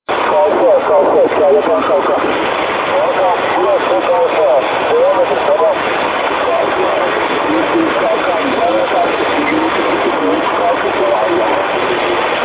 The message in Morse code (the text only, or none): none